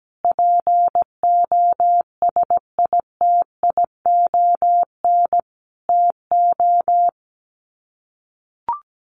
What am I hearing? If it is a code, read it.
Morse code, heard as a keyed tone, about 17 wpm, 700 Hz. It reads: POSITION TO
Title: Sets of 5 Words Encore2 17wpm